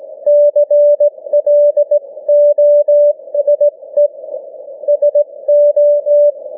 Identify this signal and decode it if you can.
Morse code, heard as a keyed tone, about 18 wpm, 590 Hz. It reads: CLOSE SO
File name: cw / ssn 40M